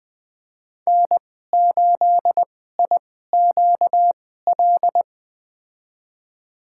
Morse code, heard as a keyed tone, about 20 wpm, spaced as for 15 wpm, 700 Hz. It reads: N8IQL
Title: USA Call Signs - 2x Character Spacing - 20wpm